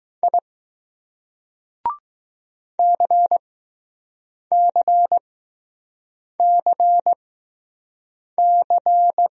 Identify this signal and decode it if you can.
Morse code, heard as a keyed tone, about 20 wpm, spaced as for 6 wpm, 700 Hz. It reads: I CCCC